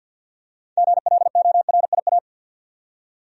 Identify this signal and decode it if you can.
Morse code, heard as a keyed tone, about 50 wpm, 700 Hz. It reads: GLORIA
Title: USA Common Names - 50wpm